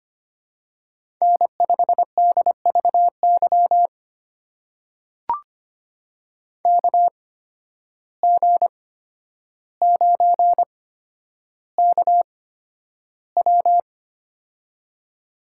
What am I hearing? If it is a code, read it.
Morse code, heard as a keyed tone, about 25 wpm, spaced as for 11 wpm, 700 Hz. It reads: N5DVY KG9KW
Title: USA Call Signs - 5 Characters - 8x Character Spacing - 25wpm